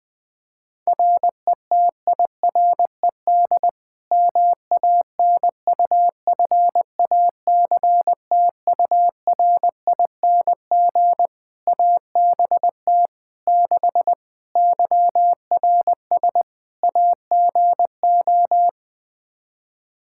Morse code, heard as a keyed tone, about 21 wpm, 700 Hz. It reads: RETIRED MANUFACTURING ABT 6 YRS AGO
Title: Practice QSO - Exchange 3 - More Info - 20wpm